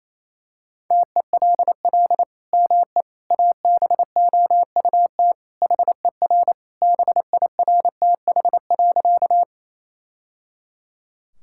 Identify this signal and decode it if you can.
Morse code, heard as a keyed tone, about 28 wpm, 700 Hz. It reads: TELL ME ABOUT HER BIRTH.
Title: Sentences from Top 1000 Words - Review Part 5 - 28wpm